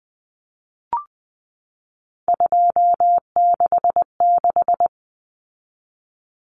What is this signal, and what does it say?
Morse code, heard as a keyed tone, about 20 wpm, 700 Hz. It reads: 266
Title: Common Cut Numbers - 3-Digit 20wpm